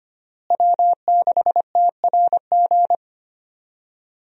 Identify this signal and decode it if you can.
Morse code, heard as a keyed tone, about 25 wpm, 700 Hz. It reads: W6TRG
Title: USA Call Signs - 5 Characters - Repeated 2x - 25wpm